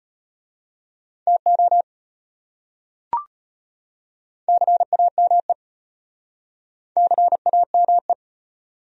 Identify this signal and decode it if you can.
Morse code, heard as a keyed tone, about 34 wpm, 700 Hz. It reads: TO CAME CAME